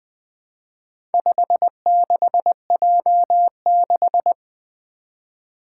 Morse code, heard as a keyed tone, about 20 wpm, 700 Hz. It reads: 56J6
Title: Four Letter-Number Combinations - 20wpm